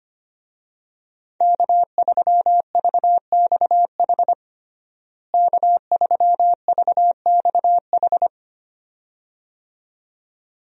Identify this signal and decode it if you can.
Morse code, heard as a keyed tone, about 25 wpm, 700 Hz. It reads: K3VXH K3VXH
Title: USA Call Signs - 5 Characters - Repeated 2x - 25wpm